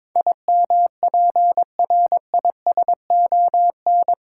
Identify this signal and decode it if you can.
Morse code, heard as a keyed tone, about 22 wpm, 700 Hz. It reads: IMPRISON
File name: practice / 22wpm / Eight Letter Words - 22wpm